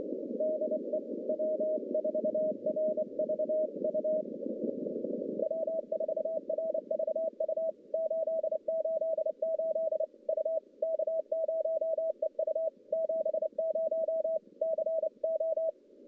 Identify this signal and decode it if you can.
Morse code, heard as a keyed tone, about 29 wpm, 640 Hz. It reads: DE W4RVU W4RVU 888 U K0EU 70 CO
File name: contests / CW-SS-k5tr- -1100